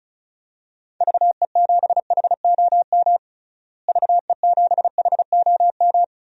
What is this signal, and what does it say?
Morse code, heard as a keyed tone, about 36 wpm, 700 Hz. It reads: VE7HOM VE7HOM
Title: DXCC-100 Call Signs - 6 Characters - Repeated 3x - 35wpm